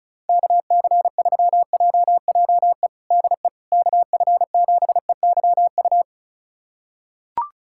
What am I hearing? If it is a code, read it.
Morse code, heard as a keyed tone, about 36 wpm, 700 Hz. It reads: KC3JJE DE KF7EYU
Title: USA Callsign DE Callsign 35wpm